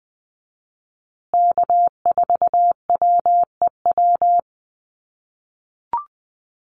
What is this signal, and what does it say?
Morse code, heard as a keyed tone, about 20 wpm, 700 Hz. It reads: K4WEW
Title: USA Call Signs Repeated 2x 20wpm